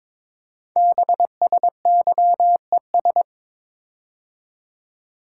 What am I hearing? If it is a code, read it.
Morse code, heard as a keyed tone, about 23 wpm, 700 Hz. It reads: BSYES
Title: Code Groups - 5 Letters - 22wpm